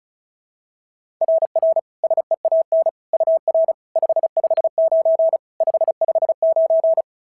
Morse code, heard as a keyed tone, about 36 wpm, 660 Hz. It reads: RR SEAN UR 559 559
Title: SOTA QSO Practice 35wpm